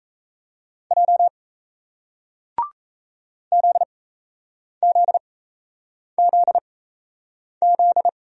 Morse code, heard as a keyed tone, about 37 wpm, spaced as for 8 wpm, 700 Hz. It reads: J ZZZZ